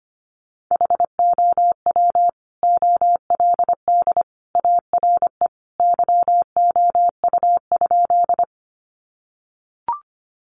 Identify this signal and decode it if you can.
Morse code, heard as a keyed tone, about 25 wpm, 700 Hz. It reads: HOW OLD ARE YOU?
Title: Sentences from Top 200 Words 25wpm